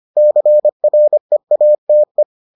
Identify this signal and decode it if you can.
Morse code, heard as a keyed tone, about 25 wpm, 600 Hz. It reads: CREATE